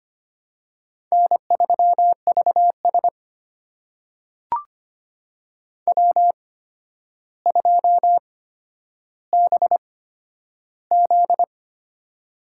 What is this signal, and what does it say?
Morse code, heard as a keyed tone, about 25 wpm, spaced as for 11 wpm, 700 Hz. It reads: N3VS W2BZ
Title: USA Call Signs - 4 Characters - 8x Character Spacing - 25wpm